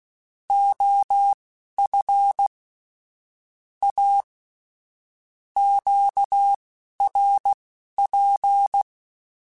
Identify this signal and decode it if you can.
Morse code, heard as a keyed tone, about 16 wpm, spaced as for 12 wpm, 800 Hz. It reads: OF A QRP
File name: QSOs / QSOs-mp3s / WD0BGP 1@15